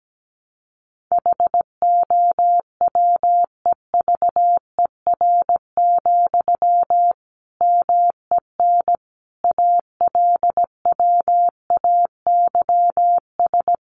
Morse code, heard as a keyed tone, about 17 wpm, 700 Hz. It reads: HOWEVER, MEN ALWAYS
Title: Sentences from Top 600 Words 17wpm